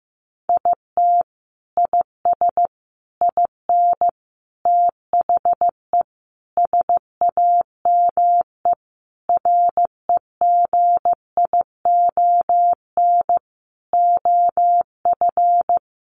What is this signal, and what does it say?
Morse code, heard as a keyed tone, about 15 wpm, 700 Hz. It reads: IT IS IN THE SAME REGION OF
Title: Sentences from Top 1500 Words 15wpm